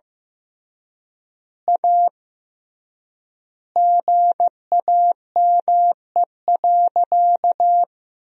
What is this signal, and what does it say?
Morse code, heard as a keyed tone, about 15 wpm, 700 Hz. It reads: A GAME.
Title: 3x Word-Spacing - Sentences from Top 500 Words - 15wpm